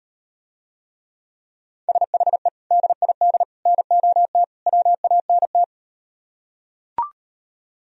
Morse code, heard as a keyed tone, about 38 wpm, 700 Hz. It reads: SHE DID NOT WANT